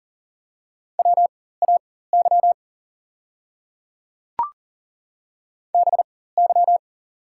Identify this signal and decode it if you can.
Morse code, heard as a keyed tone, about 40 wpm, spaced as for 19 wpm, 700 Hz. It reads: WAY BY
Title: 4x Character Spacing - Top 100 Words - 40wpm